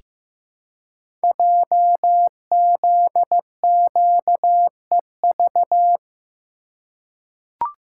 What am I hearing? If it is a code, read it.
Morse code, heard as a keyed tone, about 15 wpm, 700 Hz. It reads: JZQEV
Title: Code Groups - 5 Letters - 15wpm